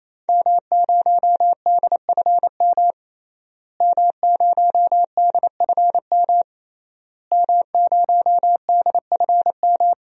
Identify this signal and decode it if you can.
Morse code, heard as a keyed tone, about 27 wpm, 700 Hz. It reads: M0DFM M0DFM M0DFM
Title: DXCC-100 Call Signs - 5 Characters - Repeated 3x - 28wpm